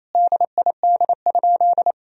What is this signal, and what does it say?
Morse code, heard as a keyed tone, about 28 wpm, 700 Hz. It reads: DID?
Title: Sentences from Top 100 Words and Abbreviations 28wpm